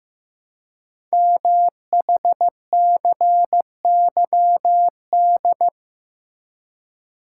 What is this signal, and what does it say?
Morse code, heard as a keyed tone, about 15 wpm, 700 Hz. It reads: MHCYD
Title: Code Groups - 5 Letters - 15wpm